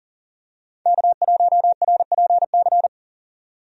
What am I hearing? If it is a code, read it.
Morse code, heard as a keyed tone, about 40 wpm, 700 Hz. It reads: K1RPC
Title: USA Call Signs - 5 Characters - 40wpm